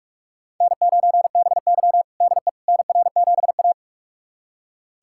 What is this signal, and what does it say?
Morse code, heard as a keyed tone, about 46 wpm, 700 Hz. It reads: N9BY DE NR7A
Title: USA Callsign DE Callsign - 4 Characters - 45wpm